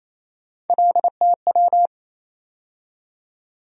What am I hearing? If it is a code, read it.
Morse code, heard as a keyed tone, about 28 wpm, 700 Hz. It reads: LTW
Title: USA Call Sign Postfixes - 3 Letters 28wpm